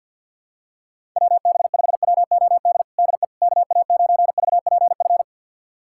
Morse code, heard as a keyed tone, about 50 wpm, 700 Hz. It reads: WB5WOD DE KA9VPF
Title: USA Callsign DE Callsign - 6 Characters - 50wpm